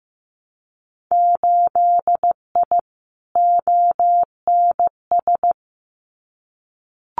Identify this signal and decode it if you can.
Morse code, heard as a keyed tone, about 15 wpm, 700 Hz. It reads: 8I ONS
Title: Winter Field Day Exchanges Encore 15wpm